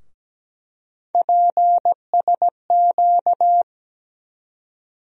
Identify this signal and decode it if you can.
Morse code, heard as a keyed tone, about 17 wpm, 700 Hz. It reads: PSQ
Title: USA Call Sign Postfixes - 3 Letters 17wpm